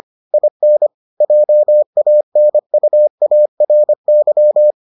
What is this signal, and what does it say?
Morse code, heard as a keyed tone, about 25 wpm, 600 Hz. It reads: IN JANUARY